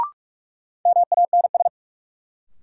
Morse code, heard as a keyed tone, about 46 wpm, 700 Hz. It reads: MANS
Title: Sentences from Top 1000 Words - Review Part 6 - New Word Drill - 45wpm